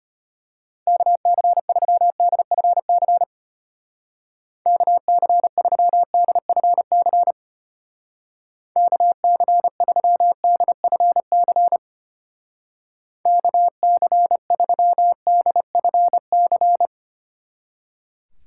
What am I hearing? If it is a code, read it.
Morse code, heard as a keyed tone, about 32 wpm, 700 Hz. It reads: KC3DFC KC3DFC KC3DFC KC3DFC